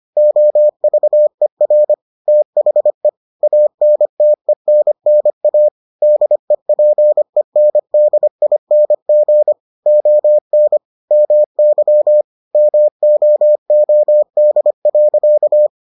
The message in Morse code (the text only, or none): OVER THE ANTENNA DEPENDING ON MY MOOD.